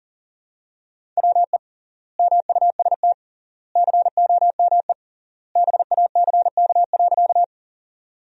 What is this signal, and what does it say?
Morse code, heard as a keyed tone, about 40 wpm, 700 Hz. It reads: WE MUST COME BACK.
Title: 3x Word-Spacing - Sentences from Top 200 Words - 40wpm